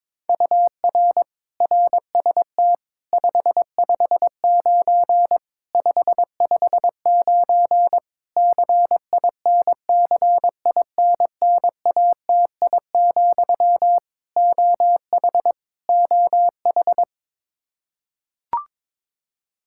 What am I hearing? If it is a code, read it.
Morse code, heard as a keyed tone, about 22 wpm, 700 Hz. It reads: UR RST 559 559 CINCINNATI, OH OH
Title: Practice QSO - Exchange 1 - Beginning - 22wpm